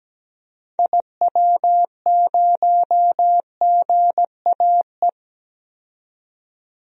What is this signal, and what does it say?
Morse code, heard as a keyed tone, about 17 wpm, 700 Hz. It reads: IW0GAE
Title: DXCC-100 Call Signs - 6 Characters - 17wpm